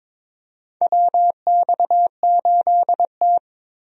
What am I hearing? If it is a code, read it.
Morse code, heard as a keyed tone, about 22 wpm, 700 Hz. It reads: WX8T
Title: USA Call Signs - 4 Characters - Repeated 3x - 22wpm